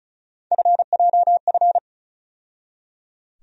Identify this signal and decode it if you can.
Morse code, heard as a keyed tone, about 36 wpm, 700 Hz. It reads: FJF
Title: USA Call Sign Postfixes - 3 Letters 35wpm